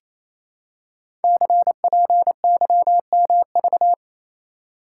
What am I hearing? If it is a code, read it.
Morse code, heard as a keyed tone, about 29 wpm, 700 Hz. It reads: CPYMV
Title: Code Groups - 5 Letter:Numbers - 28wpm